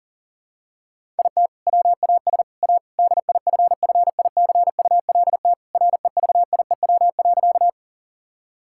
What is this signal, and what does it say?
Morse code, heard as a keyed tone, about 40 wpm, 700 Hz. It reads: IT WAS A DIFFICULT REVIEW.
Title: Sentences from Top 1000 Words - Review Part 6 - 40wpm